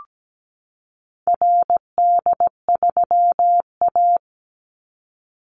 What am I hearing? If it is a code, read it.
Morse code, heard as a keyed tone, about 17 wpm, 700 Hz. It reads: RD3A
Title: Call Signs Repeated 2x - 17wpm